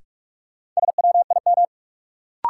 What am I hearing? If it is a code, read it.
Morse code, heard as a keyed tone, about 45 wpm, 700 Hz. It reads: SWIM
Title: Sentences from Top 1000 Words - Review Part 6 - New Word Drill - 45wpm